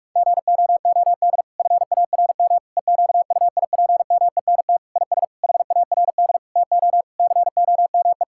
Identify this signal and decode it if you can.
Morse code, heard as a keyed tone, about 46 wpm, 700 Hz. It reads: GOOD FARM EQUIPMENT IS HARD TO COME
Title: Sentences from Top 1000 Words - Review Part 5 - 45wpm